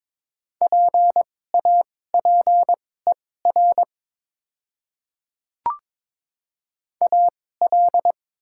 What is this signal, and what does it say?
Morse code, heard as a keyed tone, about 21 wpm, spaced as for 16 wpm, 700 Hz. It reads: PAPER AL
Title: 2x Character Spacing - Top 300 Words - 22wpm